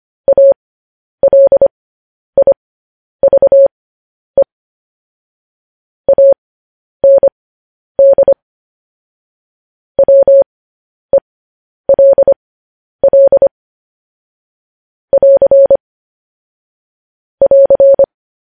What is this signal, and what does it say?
Morse code, heard as a keyed tone, about 25 wpm, spaced as for 10 wpm, 550 Hz. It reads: ALIVE AND WELL <AR> <AR>
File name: podcast / News headlines-1x-25x10